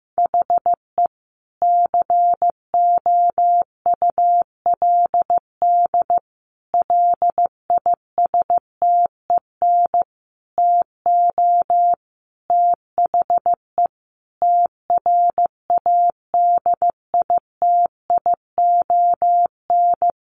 Morse code, heard as a keyed tone, about 15 wpm, 700 Hz. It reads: HE COULD LISTEN TO THE TRADITION
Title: Sentences from Top 1500 Words 15wpm